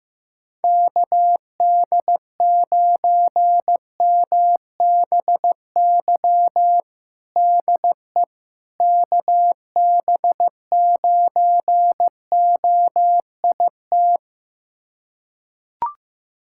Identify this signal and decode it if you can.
Morse code, heard as a keyed tone, about 15 wpm, 700 Hz. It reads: KD9MBY DE KB9OIT
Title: USA Callsign DE Callsign 15wpm